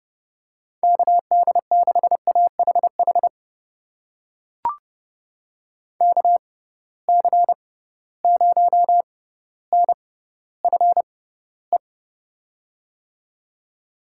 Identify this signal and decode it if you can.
Morse code, heard as a keyed tone, about 29 wpm, spaced as for 15 wpm, 700 Hz. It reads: KD6AHH KC0NFE
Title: USA Call Signs - 6x Character Spacing - 30wpm